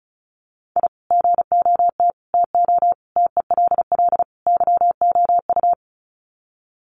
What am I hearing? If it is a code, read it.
Morse code, heard as a keyed tone, about 34 wpm, 700 Hz. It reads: I GOT TO TELL YOU
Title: Sets of 5 Words Encore2 35wpm